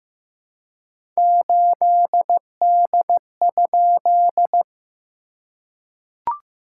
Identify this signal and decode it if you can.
Morse code, heard as a keyed tone, about 14 wpm, 700 Hz. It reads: 8D?
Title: SOTA QSO Practice 15wpm